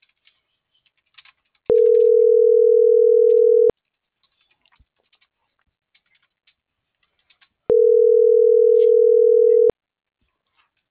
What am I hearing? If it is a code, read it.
Ringback tone (North America), 440 + 480 Hz.
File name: CENTRAL MAYAB / out raulcristinomayab